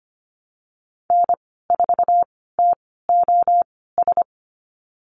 Morse code, heard as a keyed tone, about 25 wpm, spaced as for 16 wpm, 700 Hz. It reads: N4TOS
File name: Lesson 15g - Number 5 - Call Signs 25wpm